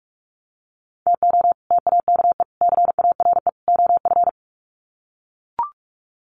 Morse code, heard as a keyed tone, about 44 wpm, 700 Hz. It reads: TO TAKE CARE OF
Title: Sets of 4 Words 45wpm